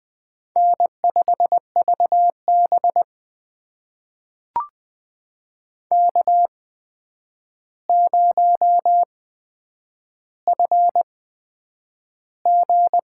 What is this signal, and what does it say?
Morse code, heard as a keyed tone, about 20 wpm, spaced as for 9 wpm, 700 Hz. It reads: N5VB K0FG
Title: USA Call Signs - 4 Characters - 8x Character Spacing - 20wpm